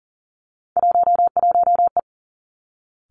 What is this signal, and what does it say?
Morse code, heard as a keyed tone, about 42 wpm, 700 Hz. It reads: 11E
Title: Three Challenging Letter-Number Combinations 40wpm